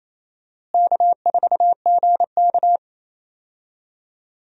Morse code, heard as a keyed tone, about 28 wpm, 700 Hz. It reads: K4GK
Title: USA Call Signs - 4 Characters - Repeated 3x - 28wpm